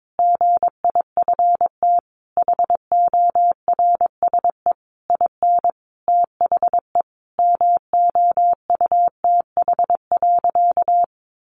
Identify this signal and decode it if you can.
Morse code, heard as a keyed tone, about 22 wpm, 700 Hz. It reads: GIFT HORSE IN THE MOUTH.
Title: Sentences from Top 1500 Words 22wpm